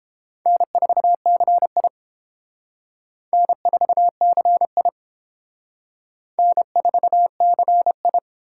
Morse code, heard as a keyed tone, about 29 wpm, 700 Hz. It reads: N4CI N4CI N4CI